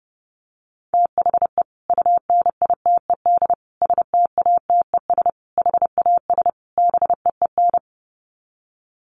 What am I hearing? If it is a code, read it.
Morse code, heard as a keyed tone, about 30 wpm, 700 Hz. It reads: THE UNITED STATES HAS BEEN
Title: Sets of 5 Words Encore2 30wpm